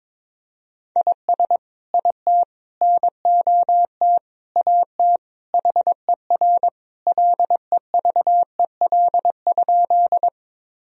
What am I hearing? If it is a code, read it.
Morse code, heard as a keyed tone, about 22 wpm, 700 Hz. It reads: IS IT NOT AT HER LEVEL?
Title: Sentences from Top 1000 Words - Review Part 4 - 22wpm